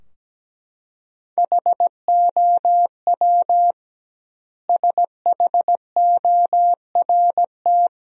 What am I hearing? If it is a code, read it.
Morse code, heard as a keyed tone, about 17 wpm, 700 Hz. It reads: HOW SHORT
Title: 2x Word-Spacing - Sentences from Top 400 Words - 17wpm